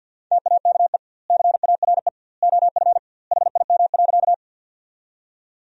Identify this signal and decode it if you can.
Morse code, heard as a keyed tone, about 50 wpm, 700 Hz. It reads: TAKE CARE OF HIM.
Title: 2x Word-Spacing - Sentences from Top 300 Words - 50wpm